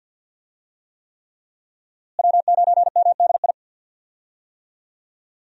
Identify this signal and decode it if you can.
Morse code, heard as a keyed tone, about 49 wpm, 700 Hz. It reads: W9MDI
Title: USA Call Signs - 5 Characters - Repeated 3x - 50wpm